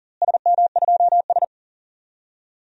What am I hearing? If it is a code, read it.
Morse code, heard as a keyed tone, about 39 wpm, 700 Hz. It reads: SM2S
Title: DXCC-100 Call Signs - 4 Characters - Repeated 2x - 40wpm